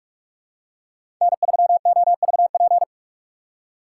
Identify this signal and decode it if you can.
Morse code, heard as a keyed tone, about 46 wpm, 700 Hz. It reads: N3OVP
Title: USA Call Signs - 3x Character Spacing - 45wpm